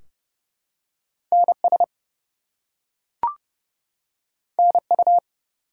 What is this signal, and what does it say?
Morse code, heard as a keyed tone, about 28 wpm, 700 Hz. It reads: NS NU
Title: US States and Canadian Provinces 30wpm